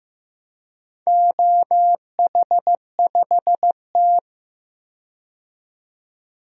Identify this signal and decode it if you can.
Morse code, heard as a keyed tone, about 15 wpm, 700 Hz. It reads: OH5T
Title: DXCC-100 Call Signs - 4 Characters - 15wpm